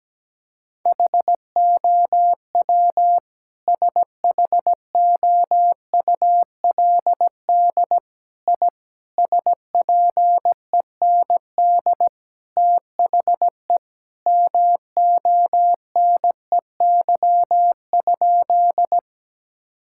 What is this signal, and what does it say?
Morse code, heard as a keyed tone, about 17 wpm, 700 Hz. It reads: HOW SHOULD I SPEND THE MONEY?
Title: Sentences from Top 1000 Words - Review Part 5 - 17wpm